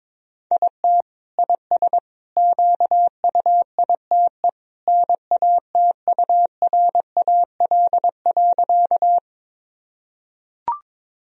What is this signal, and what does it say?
Morse code, heard as a keyed tone, about 22 wpm, 700 Hz. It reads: IT IS QUITE NATURAL.
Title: Sentences from Top 1000 Words - Review Part 4 - 22wpm